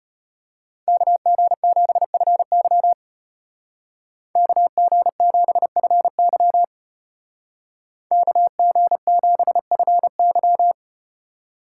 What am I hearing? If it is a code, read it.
Morse code, heard as a keyed tone, about 32 wpm, 700 Hz. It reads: KG7FY KG7FY KG7FY